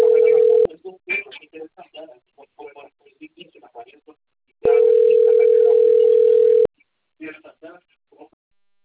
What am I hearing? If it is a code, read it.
Ringback tone (North America), 440 + 480 Hz, with speech.
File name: rg-1600-unknown- - - .53723